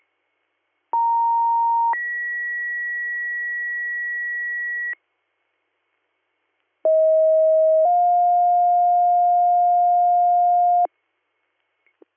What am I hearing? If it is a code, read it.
Two-tone page, heard as a one-second tone, then a three-second tone, about 930 then 1950 Hz, about 640 then 710 Hz. It is heard from 0.9 to 10.9 s.